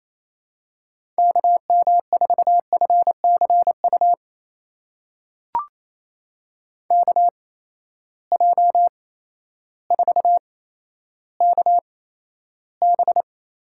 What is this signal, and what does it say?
Morse code, heard as a keyed tone, about 28 wpm, spaced as for 13 wpm, 700 Hz. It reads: KM4FCU KJ4KB
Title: USA Call Signs - 6 Characters - 8x Character Spacing - 28wpm